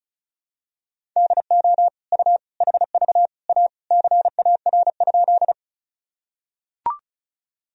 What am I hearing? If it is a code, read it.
Morse code, heard as a keyed tone, about 34 wpm, 700 Hz. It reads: DO U HV A CAR?
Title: Sentences from Top 300 Words and Abbreviations 35wpm